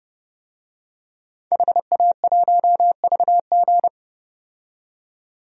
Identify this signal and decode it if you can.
Morse code, heard as a keyed tone, about 30 wpm, 700 Hz. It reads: HA1VG
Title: DXCC-100 Call Signs - 5 Characters - 30wpm